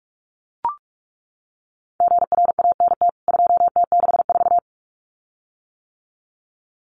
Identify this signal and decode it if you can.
Morse code, heard as a keyed tone, about 44 wpm, 700 Hz. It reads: GRANT 2T64
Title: CWT Exchange 45wpm